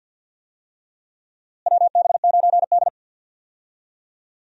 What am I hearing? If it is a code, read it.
Morse code, heard as a keyed tone, about 52 wpm, 700 Hz. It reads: WB9D
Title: USA Call Signs - 4 Characters - Repeated 2x - 50wpm